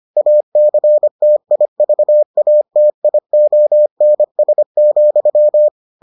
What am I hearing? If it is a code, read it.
Morse code, heard as a keyed tone, about 25 wpm, 600 Hz. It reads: ACTIVATIONS,